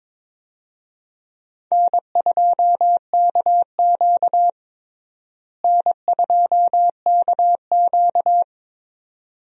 Morse code, heard as a keyed tone, about 22 wpm, 700 Hz. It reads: N2KQ N2KQ
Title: USA Call Signs - 4 Characters - Repeated 2x - 22wpm